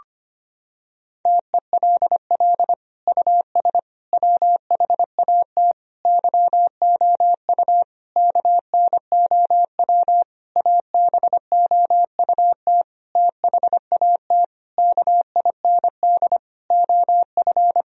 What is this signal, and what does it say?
Morse code, heard as a keyed tone, about 25 wpm, 700 Hz. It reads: TELL US WHAT YOU KNOW ABOUT THAT KIND OF
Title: Sentences from Top 1000 Words - Review Part 4 - 25wpm